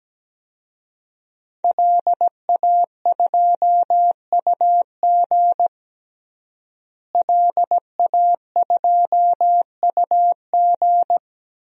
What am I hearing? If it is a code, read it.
Morse code, heard as a keyed tone, about 17 wpm, 700 Hz. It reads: LA2UG LA2UG
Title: DXCC-100 Call Signs - 5 Characters - Repeated 2x - 17wpm